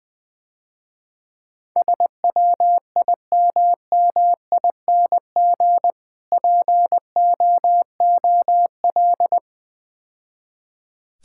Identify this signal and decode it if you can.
Morse code, heard as a keyed tone, about 20 wpm, 700 Hz. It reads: SWIMMING POOL
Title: Easy Words - Speed-Racing with Hints - 20wpm